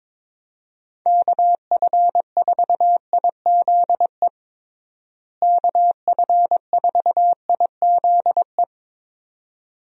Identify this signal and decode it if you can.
Morse code, heard as a keyed tone, about 22 wpm, 700 Hz. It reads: KF4IZE KF4IZE
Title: USA Call Signs - 6 Characters - Repeated 2x - 22wpm